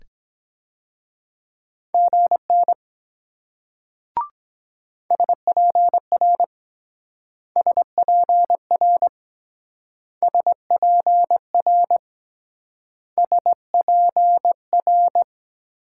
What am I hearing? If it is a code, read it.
Morse code, heard as a keyed tone, about 23 wpm, 700 Hz. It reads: GN SPR SPR SPR SPR